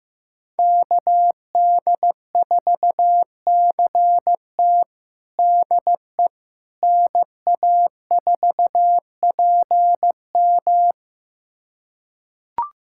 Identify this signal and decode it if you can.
Morse code, heard as a keyed tone, about 15 wpm, 700 Hz. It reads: KD4CT DE NA4PM
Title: USA Callsign DE Callsign - 5 Characters - 15wpm